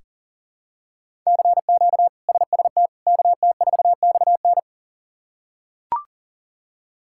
Morse code, heard as a keyed tone, about 40 wpm, 700 Hz. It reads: CQ SST KT4XN
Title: K1USN SST QSO Practice - 40wpm